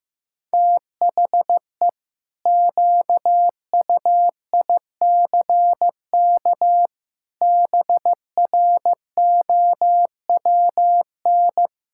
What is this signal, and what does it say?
Morse code, heard as a keyed tone, about 16 wpm, 700 Hz. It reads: THE QUICK BROWN
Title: Pangram Challenge v2 - 15wpm